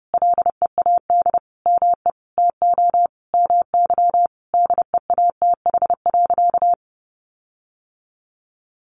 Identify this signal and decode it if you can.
Morse code, heard as a keyed tone, about 30 wpm, 700 Hz. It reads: LEAD ME TO MY DEATH.
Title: Sentences from Top 1500 Words 30wpm